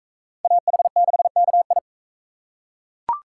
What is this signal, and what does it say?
Morse code, heard as a keyed tone, about 42 wpm, 700 Hz. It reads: AH6KI